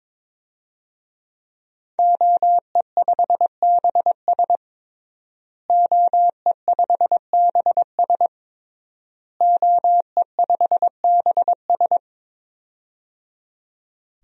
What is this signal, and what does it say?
Morse code, heard as a keyed tone, about 22 wpm, 700 Hz. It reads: OE5BS OE5BS OE5BS
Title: DXCC-100 Call Signs - 5 Characters - Repeated 3x - 22wpm